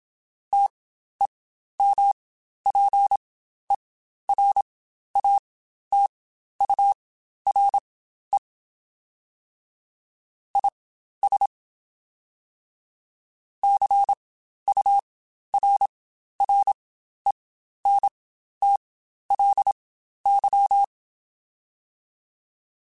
Morse code, heard as a keyed tone, about 27 wpm, spaced as for 12 wpm, 800 Hz. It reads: TEMPERATURE IS CURRENTLY
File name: Spaced Characters / Spaced Characters-mp3s / Weather5 5 sp @25